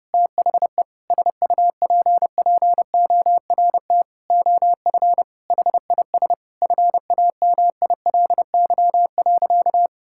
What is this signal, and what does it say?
Morse code, heard as a keyed tone, about 30 wpm, 700 Hz. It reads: THE SUPPORT OF HIS FAMILY.
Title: Sentences from Top 1000 Words - Review Part 5 - 30wpm